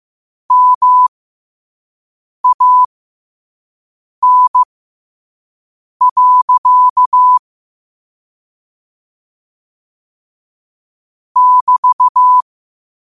Morse code, heard as a keyed tone, about 15 wpm, spaced as for 5 wpm, 1000 Hz. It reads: MAN. <BT>
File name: qotdfiles / 5wpm